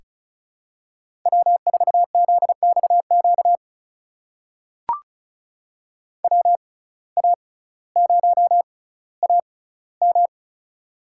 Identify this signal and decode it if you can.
Morse code, heard as a keyed tone, about 36 wpm, spaced as for 18 wpm, 700 Hz. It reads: W4ZXQ WA0AM
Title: USA Call Signs - 5 Characters - 6x Character Spacing - 35wpm